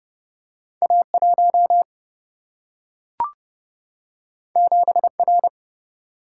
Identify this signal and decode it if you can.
Morse code, heard as a keyed tone, about 30 wpm, 700 Hz. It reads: A 1 7 R
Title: Single-Letter Single-Number 30wpm